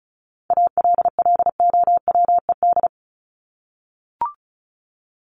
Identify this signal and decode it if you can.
Morse code, heard as a keyed tone, about 34 wpm, 700 Hz. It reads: ALLOWED
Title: ED Suffix - Top 100 Words with Base Words 35wpm